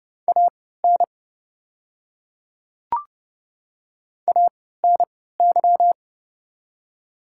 Morse code, heard as a keyed tone, about 30 wpm, spaced as for 17 wpm, 700 Hz. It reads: AN ANY
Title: 3x Character Spacing - Top 200 Words - 30wpm